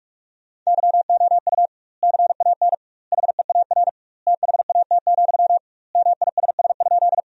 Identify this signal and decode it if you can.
Morse code, heard as a keyed tone, about 47 wpm, 700 Hz. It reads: YOU CAN HEAR THAT, MISS?
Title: 2x Word-Spacing - Sentences from Top 500 Words - 45wpm